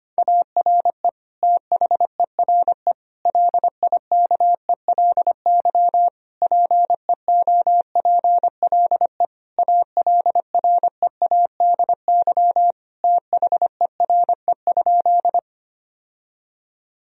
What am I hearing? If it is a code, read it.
Morse code, heard as a keyed tone, about 25 wpm, 700 Hz. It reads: ARE THERE LIKELY PEOPLE ALREADY THERE?
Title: Sentences from Top 1000 Words - Review Part 4 - 25wpm